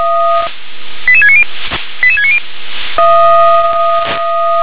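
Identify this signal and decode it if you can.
Five-tone selective call (ZVEI), heard as quick runs of five tones, tones of about 70 ms. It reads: 88690 88690
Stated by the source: Received by SDR; ZVEI five-tone call.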